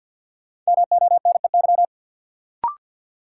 Morse code, heard as a keyed tone, about 53 wpm, 700 Hz. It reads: MONEY
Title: Sets of 4 Words - New Word Drill - 50wpm